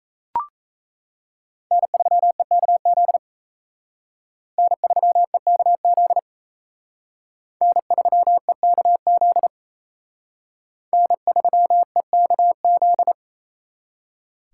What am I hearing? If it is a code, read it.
Morse code, heard as a keyed tone, about 35 wpm, 700 Hz. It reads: N3EKZ N3EKZ N3EKZ N3EKZ